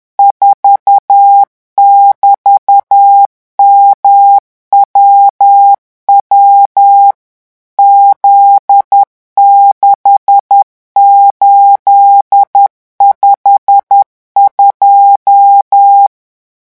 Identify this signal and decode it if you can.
Morse code, heard as a keyed tone, about 10 wpm, 800 Hz. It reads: 4<BT>MWW Z6852
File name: morse mp3 / revision13-vitesse 10 mots